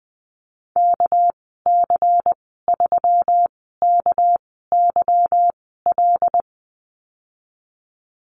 Morse code, heard as a keyed tone, about 20 wpm, spaced as for 14 wpm, 700 Hz. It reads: KC3KYL
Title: 231 Lesson 37g - Letter x - Call Signs 20wpm